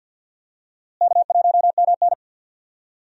Morse code, heard as a keyed tone, about 52 wpm, 700 Hz. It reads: K1MN
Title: USA Call Signs - 4 Characters - Repeated 3x - 50wpm